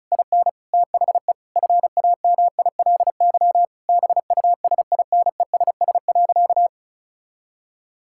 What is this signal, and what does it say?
Morse code, heard as a keyed tone, about 36 wpm, 700 Hz. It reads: IN THE FAMILY BUSINESS.
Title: Sentences from Top 1000 Words - Review Part 6 - 35wpm